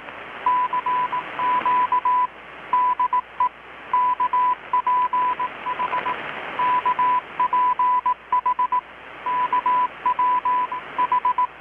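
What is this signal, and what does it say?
Morse code, heard as a keyed tone, about 18 wpm, 1000 Hz. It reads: CQ DE KPH KPH KW H
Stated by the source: Morse from KPH's Night of Nights XX maritime radiotelegraphy event, received in Chiapas, Mexico about 3,800 km away on a Kenwood R-600 with a grounded 10 m long-wire antenna.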